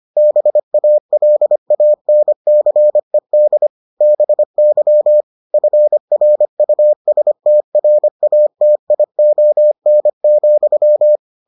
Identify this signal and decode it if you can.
Morse code, heard as a keyed tone, about 25 wpm, 600 Hz. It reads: BALANCED BY FRUSTRATION,